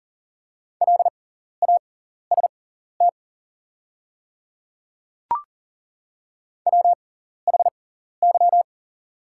Morse code, heard as a keyed tone, about 37 wpm, spaced as for 14 wpm, 690 Hz. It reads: LAST WHY
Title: 6x Character Spacing - Top 300 Words - 40wpm